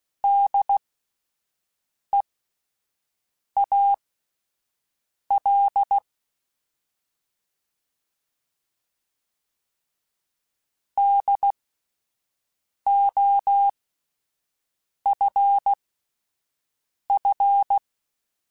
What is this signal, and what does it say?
Morse code, heard as a keyed tone, about 16 wpm, spaced as for 5 wpm, 800 Hz. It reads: DEAL DOFF